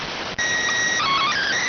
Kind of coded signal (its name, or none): SSTV (slow-scan television)